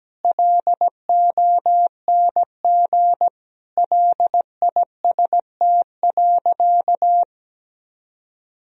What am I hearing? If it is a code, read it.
Morse code, heard as a keyed tone, about 17 wpm, 700 Hz. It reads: LONG LIST.
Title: Sentences from Top 1000 Words - Review Part 2 - 17wpm